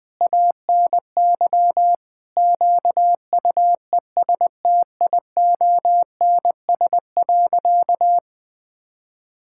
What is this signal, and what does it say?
Morse code, heard as a keyed tone, about 20 wpm, 700 Hz. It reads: ANY QUESTIONS.
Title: Sentences with 7-Word Phrases - 20wpm